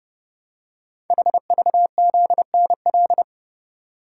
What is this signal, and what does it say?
Morse code, heard as a keyed tone, about 29 wpm, 700 Hz. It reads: HVZNL
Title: Code Groups - 5 Letters - 30wpm